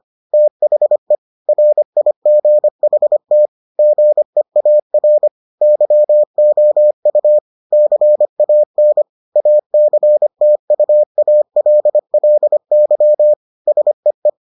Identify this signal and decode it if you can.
Morse code, heard as a keyed tone, about 25 wpm, 600 Hz. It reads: THE RIGHT GEAR YOU CAN ACTUALLY SEE